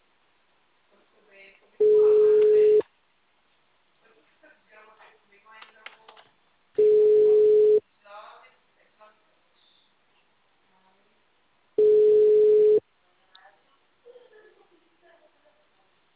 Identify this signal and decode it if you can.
Ringback tone (Europe, 425 Hz), with speech.
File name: out- -2004- - - .174